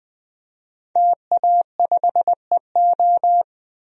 Morse code, heard as a keyed tone, about 20 wpm, 700 Hz. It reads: TA5EO
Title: DXCC-100 Call Signs - 5 Characters - Repeated 2x - 20wpm